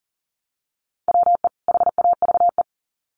Morse code, heard as a keyed tone, about 40 wpm, 700 Hz. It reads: WE HAVE
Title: Sets of 2 Words 40wpm